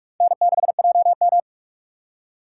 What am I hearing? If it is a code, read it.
Morse code, heard as a keyed tone, about 46 wpm, 700 Hz. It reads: N6JM
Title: USA Call Signs - 2x Character Spacing - 45wpm